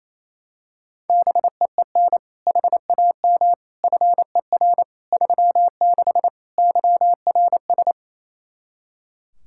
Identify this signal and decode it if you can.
Morse code, heard as a keyed tone, about 28 wpm, 700 Hz. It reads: BEEN HAM FER 36 YRS
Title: Practice QSO - Exchange 3 - More Info - 28wpm